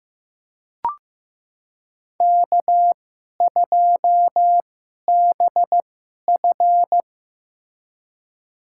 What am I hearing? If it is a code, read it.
Morse code, heard as a keyed tone, about 15 wpm, spaced as for 11 wpm, 700 Hz. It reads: K2BF
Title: USA Call Signs - 4 Characters - 2x Character Spacing - 15wpm